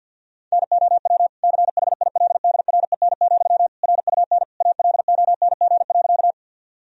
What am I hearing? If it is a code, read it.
Morse code, heard as a keyed tone, about 52 wpm, 700 Hz. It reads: NOW CHILDREN, RUN ALONG.
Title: Sentences from Top 1000 Words - Review Part 1 - 50wpm